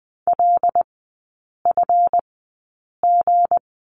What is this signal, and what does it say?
Morse code, heard as a keyed tone, about 20 wpm, spaced as for 8 wpm, 700 Hz. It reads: LFG
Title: 178 Lesson 29e - Slash - Three Characters 20wpm